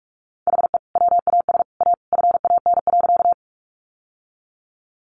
Morse code, heard as a keyed tone, about 44 wpm, 700 Hz. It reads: HE WAS A FAN.
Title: Sentences from Top 1300 Words 45wpm